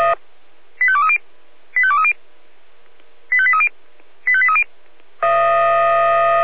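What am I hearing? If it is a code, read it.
Five-tone selective call (ZVEI), heard as quick runs of five tones, tones of about 70 ms. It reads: 85230 85230 85830 85830
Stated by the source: ZVEI five-tone call 85855 received via SDR.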